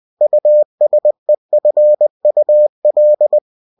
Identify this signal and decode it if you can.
Morse code, heard as a keyed tone, about 20 wpm, 600 Hz. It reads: USEFUL
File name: cw-books / winnie-the-pooh / winnie-20wpm-page-0062